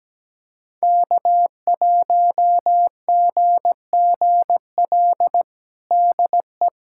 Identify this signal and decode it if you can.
Morse code, heard as a keyed tone, about 17 wpm, 700 Hz. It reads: K1GGL DE
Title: USA Callsign DE Callsign - 5 Characters - 17wpm